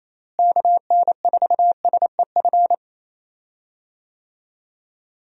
Morse code, heard as a keyed tone, about 28 wpm, 700 Hz. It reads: KN4SEF
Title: USA Call Signs - 6 Characters - Repeated 2x - 28wpm